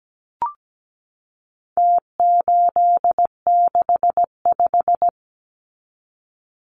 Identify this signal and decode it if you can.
Morse code, heard as a keyed tone, about 17 wpm, 700 Hz. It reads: T865
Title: Numbers - Contest Style - 4-Digit 17wpm